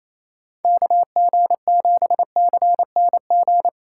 Morse code, heard as a keyed tone, about 28 wpm, 700 Hz. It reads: KG7CNG
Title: USA Call Signs - 6 Characters - 2x Character Spacing - 28wpm